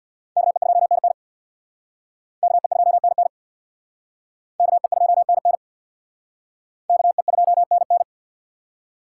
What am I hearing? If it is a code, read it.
Morse code, heard as a keyed tone, about 62 wpm, 700 Hz. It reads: KE2NN KE2NN KE2NN KE2NN
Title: Speed Racing - USA Call Signs 50wpm